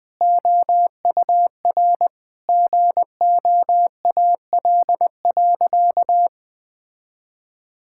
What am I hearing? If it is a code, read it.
Morse code, heard as a keyed tone, about 20 wpm, 700 Hz. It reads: OUR GOAL.
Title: Sentences with 7-Word Phrases - Spoken Beginning - 20wpm